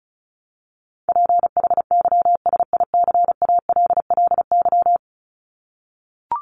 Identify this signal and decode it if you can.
Morse code, heard as a keyed tone, about 34 wpm, 700 Hz. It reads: PHYSICALLY
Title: Top 1501-1750 Words 35wpm